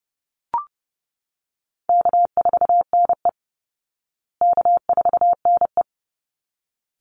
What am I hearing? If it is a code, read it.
Morse code, heard as a keyed tone, about 30 wpm, 700 Hz. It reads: K4NE K4NE
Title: Easy Call Signs Repeated 2x - 30wpm